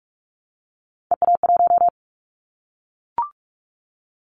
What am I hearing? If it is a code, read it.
Morse code, heard as a keyed tone, about 44 wpm, 700 Hz. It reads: EA1
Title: Callsign Prefixes for Top 25 DXCC - 45wpm